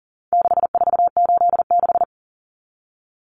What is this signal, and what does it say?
Morse code, heard as a keyed tone, about 40 wpm, 700 Hz. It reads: 6486
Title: Numbers - 4-Digit 40wpm